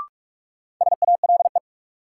Morse code, heard as a keyed tone, about 46 wpm, 700 Hz. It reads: SALE
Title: Sentences from Top 1400 Words - New Word Drill Alt - 45wpm